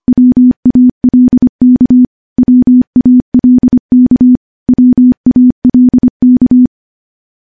Morse code, heard as a keyed tone, about 25 wpm, 250 Hz. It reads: WALK WALK WALK
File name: morse / morse-training-25WPM